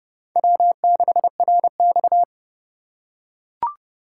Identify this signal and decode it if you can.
Morse code, heard as a keyed tone, about 29 wpm, 700 Hz. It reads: W6RX
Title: USA Call Signs - 4 Characters - 3x Character Spacing - 30wpm